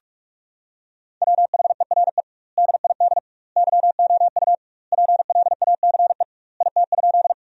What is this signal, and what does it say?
Morse code, heard as a keyed tone, about 46 wpm, 700 Hz. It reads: WHERE DID YOU PLACE IT?
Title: 2x Word-Spacing - Sentences from Top 200 Words - 45wpm